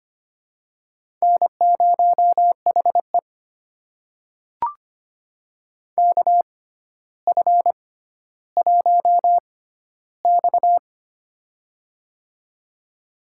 Morse code, heard as a keyed tone, about 25 wpm, spaced as for 13 wpm, 700 Hz. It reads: N0HE KF1X
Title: USA Call Signs - 4 Characters - 6x Character Spacing - 25wpm